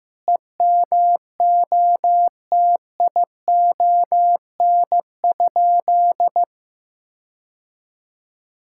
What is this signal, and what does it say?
Morse code, heard as a keyed tone, about 16 wpm, 700 Hz. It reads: EMOTION?
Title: Sentences from Top 1600 Words 15wpm